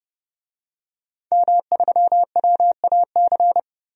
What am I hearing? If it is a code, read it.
Morse code, heard as a keyed tone, about 31 wpm, 700 Hz. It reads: M3WAC
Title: DXCC-100 Call Signs - 5 Characters - 30wpm